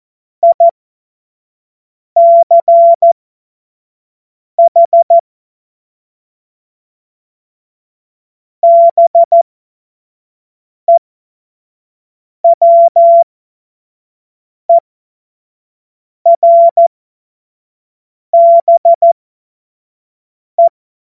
Morse code, heard as a keyed tone, about 14 wpm, spaced as for 5 wpm, 680 Hz. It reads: ICH BEWERBE
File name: Morsekurs / Witze / Witz 18. Juli 2023 25 F70